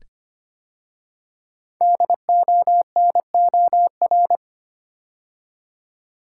Morse code, heard as a keyed tone, about 25 wpm, 700 Hz. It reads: DONOR
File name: 5 Letter Words - First 3 Letters Given - 25wpm